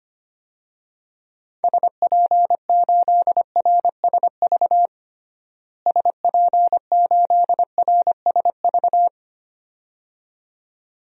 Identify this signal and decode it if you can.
Morse code, heard as a keyed tone, about 25 wpm, 700 Hz. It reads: SP8RSV SP8RSV
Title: DXCC-100 Call Signs - 6 Characters - Repeated 2x - 25wpm